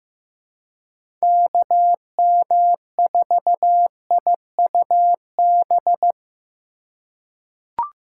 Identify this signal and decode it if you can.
Morse code, heard as a keyed tone, about 15 wpm, 700 Hz. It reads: KM4IUB
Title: USA Call Signs - 6 Characters - 4x Character Spacing - 15wpm